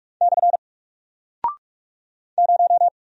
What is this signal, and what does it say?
Morse code, heard as a keyed tone, about 46 wpm, 700 Hz. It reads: / 0
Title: Single Character (Letter:Number:Punctuation:Prosign) Alt 45wpm